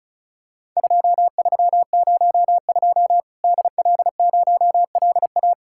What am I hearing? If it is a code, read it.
Morse code, heard as a keyed tone, about 36 wpm, 700 Hz. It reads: 2302 DL0LA
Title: Worked All Europe DX Contest - QTC Exchange - 35wpm